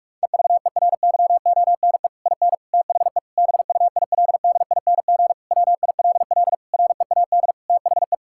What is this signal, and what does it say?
Morse code, heard as a keyed tone, about 43 wpm, 700 Hz. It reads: EVERYONE IN THE BUILDING WILL READ THE
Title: Sentences from Top 1000 Words - Review Part 6 - 45wpm